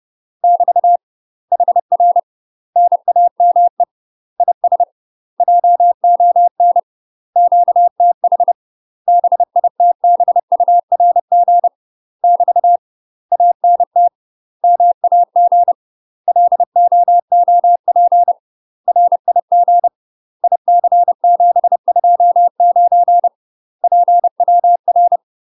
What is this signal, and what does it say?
Morse code, heard as a keyed tone, about 30 wpm, 700 Hz. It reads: <BT> HR NAME IS JON QTH BITBURG <BT> ANT MAG LOOP RIG IC729 PWR
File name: UK MASTER QSOs TRAINING / 30WPM / UK MASTER QSOs 0005